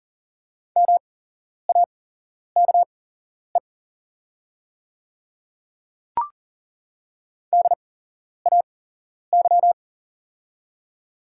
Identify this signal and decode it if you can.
Morse code, heard as a keyed tone, about 40 wpm, spaced as for 11 wpm, 690 Hz. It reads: MAKE DAY
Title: 8x Character Spacing - Top 300 Words - 40wpm